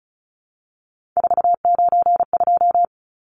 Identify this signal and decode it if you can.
Morse code, heard as a keyed tone, about 34 wpm, 700 Hz. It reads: 492
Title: Numbers - 3-Digit 35wpm